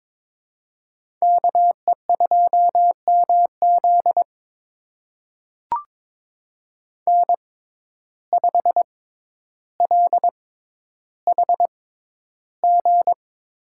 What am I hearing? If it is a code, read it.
Morse code, heard as a keyed tone, about 21 wpm, spaced as for 11 wpm, 700 Hz. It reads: KE2MZ N5LHG
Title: USA Call Signs - 5 Characters - 6x Character Spacing - 22wpm